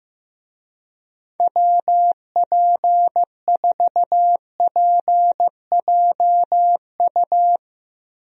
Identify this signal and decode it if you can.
Morse code, heard as a keyed tone, about 16 wpm, 700 Hz. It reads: WP4PJU
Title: USA Call Signs - 6 Characters - Repeated 2x - 15wpm